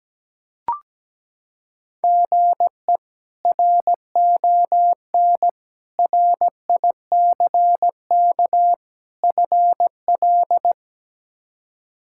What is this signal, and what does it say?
Morse code, heard as a keyed tone, about 17 wpm, 700 Hz. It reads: GE RON RICK FL
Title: K1USN SST QSO Practice - 17wpm